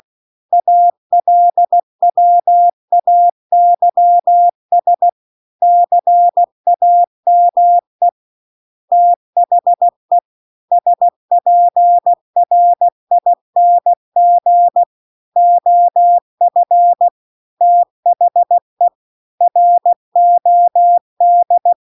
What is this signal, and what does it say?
Morse code, heard as a keyed tone, about 16 wpm, 700 Hz. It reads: ALWAYS CAME THE SPRING OF THE ROD